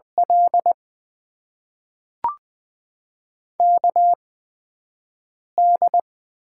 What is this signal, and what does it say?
Morse code, heard as a keyed tone, about 20 wpm, spaced as for 4 wpm, 700 Hz. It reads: LKD
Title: USA Call Signs - 8x Character Spacing - 20wpm